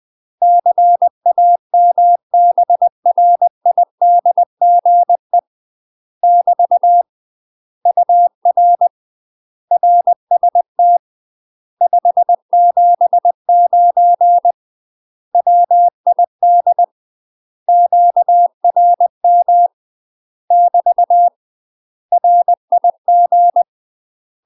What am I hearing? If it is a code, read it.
Morse code, heard as a keyed tone, about 20 wpm, 700 Hz. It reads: CAMBRIDGE <BT> UR RST 579 WID QRM <BT> RIG
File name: UK MASTER QSOs TRAINING / 20WPM / UK MASTER QSOs 0059